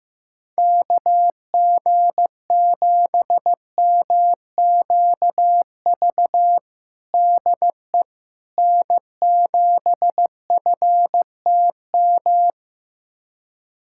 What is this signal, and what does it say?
Morse code, heard as a keyed tone, about 16 wpm, 700 Hz. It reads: KG7MQV DE N7FTM
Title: USA Callsign DE Callsign 15wpm